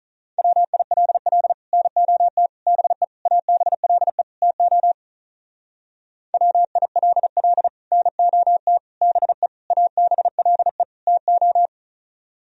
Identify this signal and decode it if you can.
Morse code, heard as a keyed tone, about 45 wpm, 700 Hz. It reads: WILL NOT BE ABLE TO WILL NOT BE ABLE TO